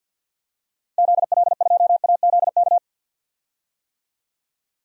Morse code, heard as a keyed tone, about 51 wpm, 700 Hz. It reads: ZL2AZK
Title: DXCC-100 Call Signs - 6 Characters - 50wpm